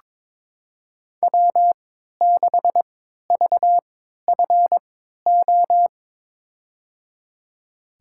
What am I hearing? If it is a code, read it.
Morse code, heard as a keyed tone, about 22 wpm, spaced as for 13 wpm, 700 Hz. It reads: W6VFO
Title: USA Call Signs - 5 Characters - 3x Character Spacing - 22wpm